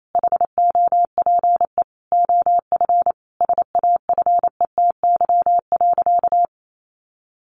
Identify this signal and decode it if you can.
Morse code, heard as a keyed tone, about 28 wpm, 700 Hz. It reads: HOPE OF SAFETY.
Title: Sentences from Top 1000 Words 28wpm